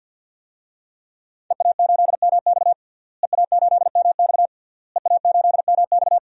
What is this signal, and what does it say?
Morse code, heard as a keyed tone, about 51 wpm, 700 Hz. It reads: EA8MX EA8MX EA8MX
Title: DXCC-100 Call Signs - 5 Characters - Repeated 3x - 50wpm